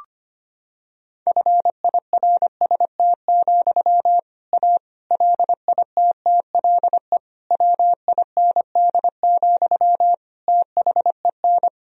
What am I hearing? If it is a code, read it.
Morse code, heard as a keyed tone, about 25 wpm, 700 Hz. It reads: FIRST, A LITTLE WIND, THEN
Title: Sentences from Top 1000 Words - Review Part 2 - 25wpm